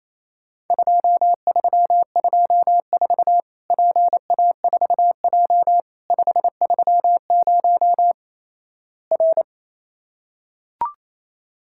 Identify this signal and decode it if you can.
Morse code, heard as a keyed tone, about 28 wpm, 700 Hz. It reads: 2324 PA4J 530 R
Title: Worked All Europe DX Contest - QTC Exchange - 28wpm